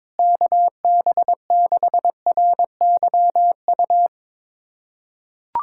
Morse code, heard as a keyed tone, about 22 wpm, 700 Hz. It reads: KB6RYU
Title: USA Callsign DE Callsign - 6 Characters - 22wpm